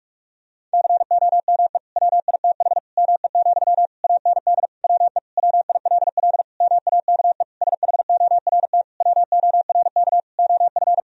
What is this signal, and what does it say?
Morse code, heard as a keyed tone, about 46 wpm, 700 Hz. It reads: COME WITH ME, AND WE WILL MAKE SHORT WORK OF
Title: Sentences from Top 1000 Words - Review Part 2 - 45wpm